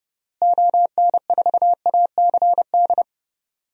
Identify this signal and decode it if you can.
Morse code, heard as a keyed tone, about 32 wpm, 700 Hz. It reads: ON4ACD
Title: DXCC-100 Call Signs - 6 Characters - Repeated 2x - 30wpm